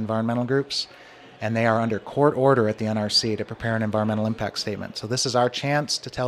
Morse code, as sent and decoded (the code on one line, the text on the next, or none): none